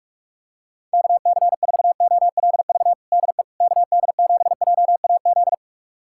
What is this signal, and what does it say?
Morse code, heard as a keyed tone, about 46 wpm, 700 Hz. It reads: KC4OLV DE KD7JAZ
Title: USA Callsign DE Callsign - 6 Characters - 45wpm